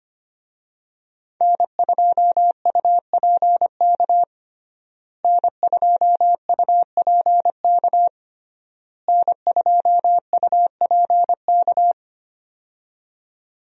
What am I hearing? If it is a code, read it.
Morse code, heard as a keyed tone, about 25 wpm, 700 Hz. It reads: N2UPK N2UPK N2UPK
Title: USA Call Signs - 5 Characters - Repeated 3x - 25wpm